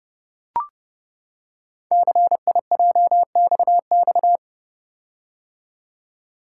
Morse code, heard as a keyed tone, about 28 wpm, 700 Hz. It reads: CIJXX
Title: Code Groups - 5 Letters - 30wpm